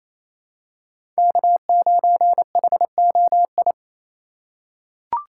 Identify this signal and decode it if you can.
Morse code, heard as a keyed tone, about 28 wpm, 700 Hz. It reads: K9HOI
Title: USA Call Signs - 4x Character Spacing - 28wpm